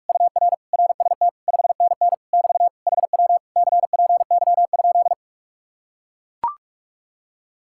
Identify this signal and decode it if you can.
Morse code, heard as a keyed tone, about 49 wpm, 700 Hz. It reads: UR RST 5NN <BT> HW CPY?